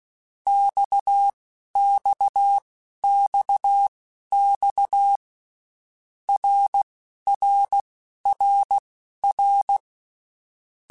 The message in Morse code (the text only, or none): XXXX RRRR